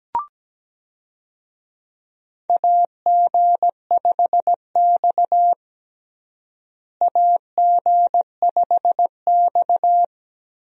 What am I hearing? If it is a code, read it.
Morse code, heard as a keyed tone, about 17 wpm, 700 Hz. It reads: AG5X AG5X
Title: USA Call Signs - 4 Characters - Repeated 3x - 17wpm